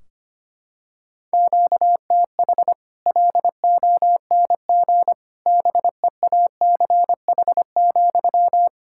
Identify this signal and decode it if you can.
Morse code, heard as a keyed tone, about 25 wpm, 700 Hz. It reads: QTH LONG BEACH,
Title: Practice QSO - Exchange 1 - Beginning - 25wpm